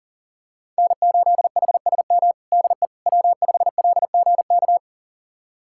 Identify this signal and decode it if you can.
Morse code, heard as a keyed tone, about 40 wpm, 700 Hz. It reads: N8HSM DE W5LGK
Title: USA Callsign DE Callsign - 5 Characters - 40wpm